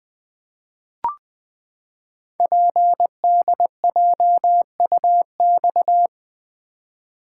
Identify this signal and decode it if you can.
Morse code, heard as a keyed tone, about 20 wpm, 700 Hz. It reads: PDJUX
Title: Code Groups - 5 Letters - 20wpm